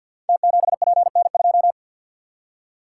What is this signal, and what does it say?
Morse code, heard as a keyed tone, about 49 wpm, 700 Hz. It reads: T7PN2
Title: Code Groups - 5 Letter:Numbers - 50wpm